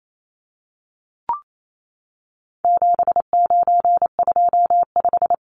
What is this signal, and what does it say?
Morse code, heard as a keyed tone, about 28 wpm, 700 Hz. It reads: 7925
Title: Numbers - 4-Digit 28wpm